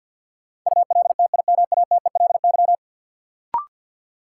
Morse code, heard as a keyed tone, about 53 wpm, 700 Hz. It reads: ULTIMATELY
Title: Sentences from Top 1200 Words - New Word Drill - 50wpm